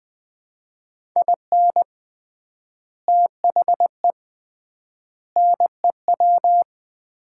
Morse code, heard as a keyed tone, about 20 wpm, 700 Hz. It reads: IN THE NEW
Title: 3x Word-Spacing - Sets of 5 Words - 20wpm